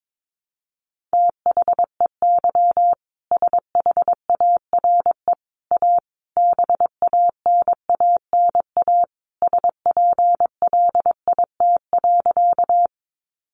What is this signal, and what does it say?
Morse code, heard as a keyed tone, about 22 wpm, 700 Hz. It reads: THEY SHARE A BANANA SPLIT.
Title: Additional Easy Phrases 22wpm